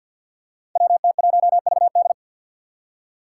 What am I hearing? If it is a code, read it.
Morse code, heard as a keyed tone, about 47 wpm, 700 Hz. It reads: WT1VD
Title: Five Challenging Letter-Number Combinations 50wpm